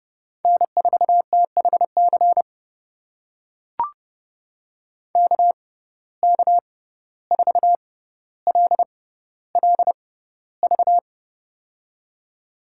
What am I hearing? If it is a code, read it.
Morse code, heard as a keyed tone, about 30 wpm, spaced as for 15 wpm, 700 Hz. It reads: N4THC KK4LLV
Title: USA Call Signs - 6x Character Spacing - 30wpm